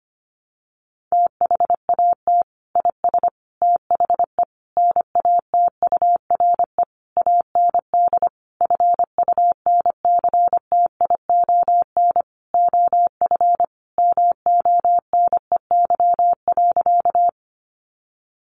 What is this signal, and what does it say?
Morse code, heard as a keyed tone, about 25 wpm, 700 Hz. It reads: THAT IS THE NATURE AND FUNCTION OF MONEY.
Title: Sentences from Top 1100 Words 25wpm